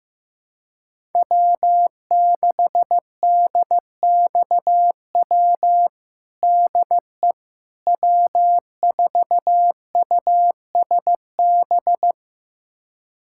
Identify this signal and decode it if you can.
Morse code, heard as a keyed tone, about 15 wpm, 700 Hz. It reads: W6DXW DE W4USB
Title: USA Callsign DE Callsign - 5 Characters - 15wpm